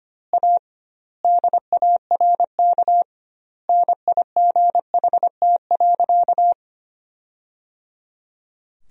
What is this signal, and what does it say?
Morse code, heard as a keyed tone, about 25 wpm, 700 Hz. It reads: A DARK NIGHT.
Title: 2x Word-Spacing - Sentences from Top 500 Words - 25wpm